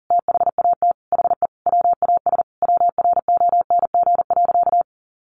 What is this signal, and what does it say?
Morse code, heard as a keyed tone, about 40 wpm, 700 Hz. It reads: THAT HE WAS WRONG.
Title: Sentences from Top 1100 Words 40wpm